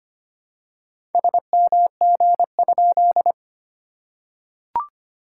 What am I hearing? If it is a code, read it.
Morse code, heard as a keyed tone, about 25 wpm, 690 Hz. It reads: SMG?
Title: SOTA QSO Practice 25wpm